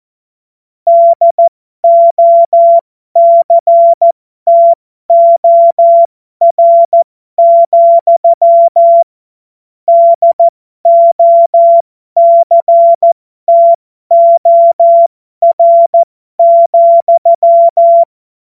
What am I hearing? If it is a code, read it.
Morse code, heard as a keyed tone, about 14 wpm, spaced as for 12 wpm, 680 Hz. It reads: DOCTOR, DOCTOR,
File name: Morsekurs / Witze